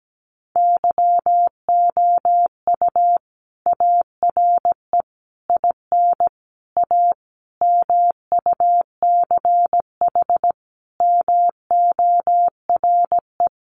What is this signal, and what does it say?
Morse code, heard as a keyed tone, about 17 wpm, 700 Hz. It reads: YOU ARE IN A MUCH MORE
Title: Sentences from Top 1400 Words 17wpm